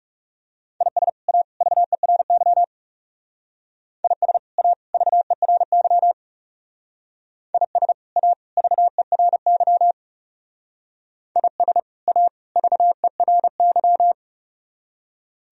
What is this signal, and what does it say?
Morse code, heard as a keyed tone, about 36 wpm, 700 Hz. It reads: IS A VERY IS A VERY IS A VERY IS A VERY
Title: Speed-Racing - Sets of 3 Words 30wpm